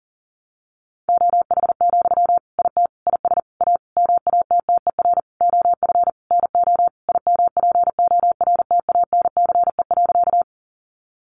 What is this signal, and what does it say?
Morse code, heard as a keyed tone, about 40 wpm, 700 Hz. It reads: OH, IT IS A MATTER OF NO IMPORTANCE.
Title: Sentences from Top 1200 Words 40wpm